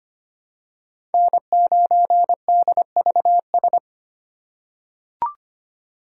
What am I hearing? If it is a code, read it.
Morse code, heard as a keyed tone, about 25 wpm, 700 Hz. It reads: N9DVS
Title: USA Call Signs - 5 Characters - 3x Character Spacing - 25wpm